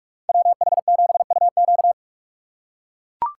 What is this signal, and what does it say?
Morse code, heard as a keyed tone, about 43 wpm, 700 Hz. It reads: WH7UQ
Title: USA Call Signs - 5 Characters - 6x Character Spacing - 45wpm